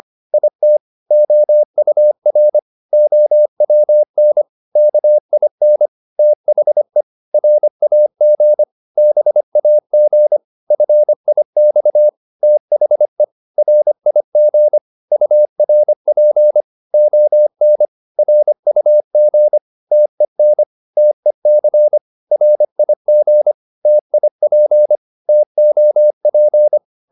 Morse code, heard as a keyed tone, about 25 wpm, 600 Hz. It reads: IT OUR OWN KIN THE RAG BAG FIX THE RIG URP ON RUG TEN TEC RIG TIP TOP
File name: PR204a 25